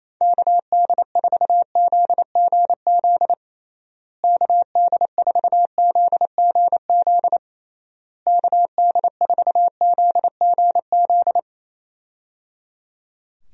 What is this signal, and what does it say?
Morse code, heard as a keyed tone, about 27 wpm, 700 Hz. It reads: KD4ZGZ KD4ZGZ KD4ZGZ
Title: USA Call Signs - 6 Characters - Repeated 3x - 28wpm